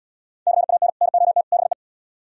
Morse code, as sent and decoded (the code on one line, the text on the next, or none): -.-. .- -. / -. --- - / -... .
CAN NOT BE